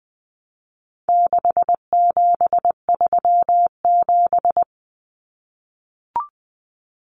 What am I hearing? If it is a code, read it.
Morse code, heard as a keyed tone, about 20 wpm, 700 Hz. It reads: 6737
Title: Common Cut Numbers - 4-Digit 20wpm